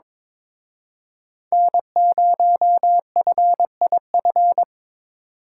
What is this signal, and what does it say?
Morse code, heard as a keyed tone, about 25 wpm, 700 Hz. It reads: N0FIF
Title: Speed Racing - USA 1x3 Call Signs 22wpm